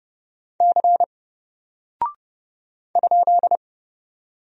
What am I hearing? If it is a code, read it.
Morse code, heard as a keyed tone, about 30 wpm, 700 Hz. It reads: C ?
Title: Single Character (Letter:Number:Punctuation:Prosign) 30wpm